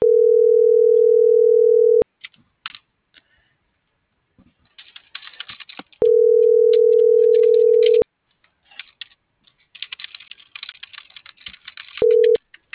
Ringback tone (North America), 440 + 480 Hz.
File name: CENTRAL FRALSE / out erik2024sub